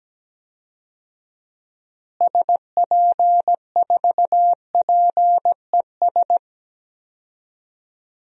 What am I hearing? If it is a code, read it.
Morse code, heard as a keyed tone, about 17 wpm, 700 Hz. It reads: SP4PES
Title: DXCC-100 Call Signs - 6 Characters - Repeated 2x - 17wpm